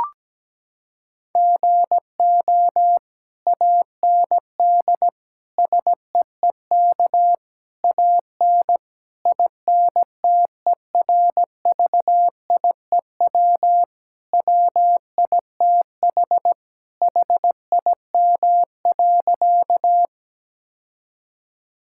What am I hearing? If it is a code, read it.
Morse code, heard as a keyed tone, about 17 wpm, 700 Hz. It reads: GO AND SEEK AN INTERVIEW WITH HIM.
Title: Sentences from Top 1600 Words 17wpm